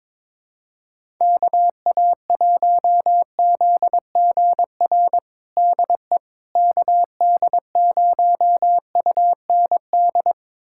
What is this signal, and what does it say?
Morse code, heard as a keyed tone, about 21 wpm, 700 Hz. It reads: KA1ZGR DE KD0UND
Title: USA Callsign DE Callsign - 6 Characters - 22wpm